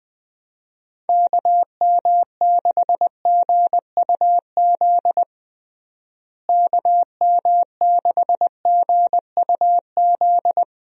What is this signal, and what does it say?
Morse code, heard as a keyed tone, about 21 wpm, 700 Hz. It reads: KM6GUZ KM6GUZ
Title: USA Call Signs - 6 Characters - Repeated 2x - 20wpm